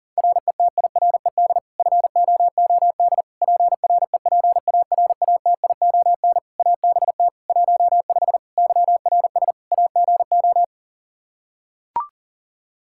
Morse code, heard as a keyed tone, about 40 wpm, 700 Hz. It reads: RETIRED FOOD PREPARATION ABT 15 YRS AGO
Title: K1USN SST QSO Practice 40wpm